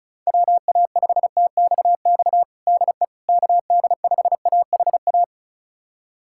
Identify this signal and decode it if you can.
Morse code, heard as a keyed tone, about 36 wpm, 700 Hz. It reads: WA5TXX DE KD5AHA
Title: USA Callsign DE Callsign - 6 Characters - 35wpm